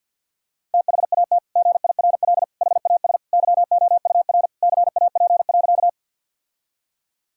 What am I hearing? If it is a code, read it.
Morse code, heard as a keyed tone, about 51 wpm, 700 Hz. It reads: THAT GIRL HAS YOUR CAP.
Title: Sentences from Top 1600 Words 50wpm